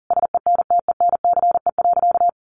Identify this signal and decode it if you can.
Morse code, heard as a keyed tone, about 40 wpm, 700 Hz. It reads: SENTENCE.
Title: Sentences from Top 1200 Words 40wpm